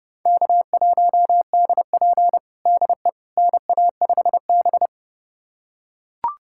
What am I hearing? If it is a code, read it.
Morse code, heard as a keyed tone, about 32 wpm, 700 Hz. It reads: K1DP DE NA5B
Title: USA Callsign DE Callsign - 4 Characters - 30wpm